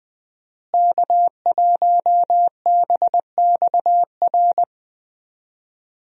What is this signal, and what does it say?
Morse code, heard as a keyed tone, about 20 wpm, 700 Hz. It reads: K1BXR
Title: USA Call Signs - 5 Characters - Repeated 3x - 20wpm